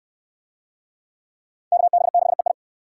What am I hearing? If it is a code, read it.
Morse code, heard as a keyed tone, about 69 wpm, 700 Hz. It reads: KB6EI